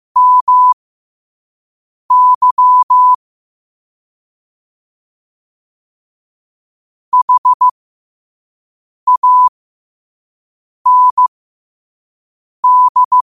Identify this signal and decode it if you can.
Morse code, heard as a keyed tone, about 15 wpm, spaced as for 5 wpm, 1000 Hz. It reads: MY HAND
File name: qotdfiles / 5wpm